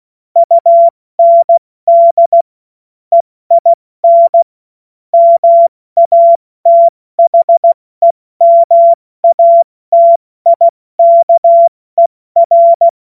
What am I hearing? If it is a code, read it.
Morse code, heard as a keyed tone, about 16 wpm, spaced as for 14 wpm, 680 Hz. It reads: UND EIN MATHEMATIKER
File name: Morsekurs / Witze / Witz 07. Dezember 2021, 70 F80